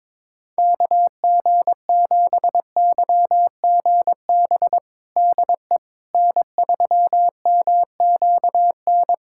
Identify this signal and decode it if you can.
Morse code, heard as a keyed tone, about 22 wpm, 700 Hz. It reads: KG7YGB DE N3MQN
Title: USA Callsign DE Callsign 22wpm